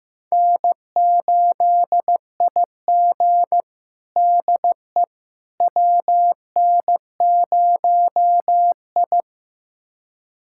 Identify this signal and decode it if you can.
Morse code, heard as a keyed tone, about 16 wpm, 700 Hz. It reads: N8IG DE WN0I
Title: USA Callsign DE Callsign - 4 Characters - 15wpm